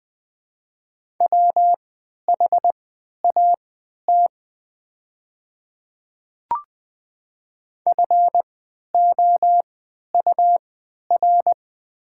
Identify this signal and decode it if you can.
Morse code, heard as a keyed tone, about 20 wpm, spaced as for 11 wpm, 690 Hz. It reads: WHAT FOUR
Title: 3x Character Spacing - Top 300 Words - 20wpm